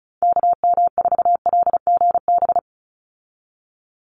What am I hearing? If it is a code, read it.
Morse code, heard as a keyed tone, about 34 wpm, 700 Hz. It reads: KM4LGB
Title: USA Call Signs 35wpm